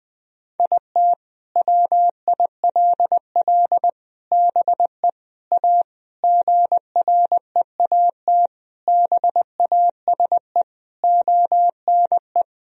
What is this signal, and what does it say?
Morse code, heard as a keyed tone, about 21 wpm, 700 Hz. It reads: IT WILL BE A GREAT BASE ONE
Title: Sentences from Top 1000 Words - Review Part 1 - 20wpm